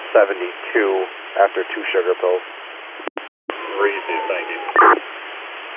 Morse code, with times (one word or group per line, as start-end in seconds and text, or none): none